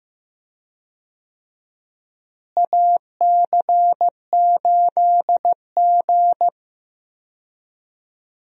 Morse code, heard as a keyed tone, about 15 wpm, 700 Hz. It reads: AC8G
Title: USA Call Signs - 4 Characters - Repeated 2x - 15wpm